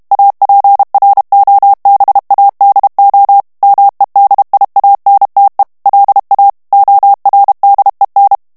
Morse code, heard as a keyed tone, about 32 wpm, 800 Hz. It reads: APROBADO MEDIANTE LA ORDEN
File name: ESPAGNOL TEXTE EN CLAIR 2 VITESSE 30 MOTS MINUTES